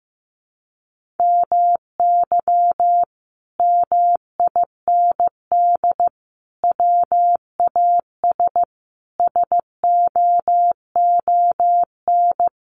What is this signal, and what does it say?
Morse code, heard as a keyed tone, about 15 wpm, 700 Hz. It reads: MY MIND WAS SOON
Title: Sentences from Top 500 Words 15wpm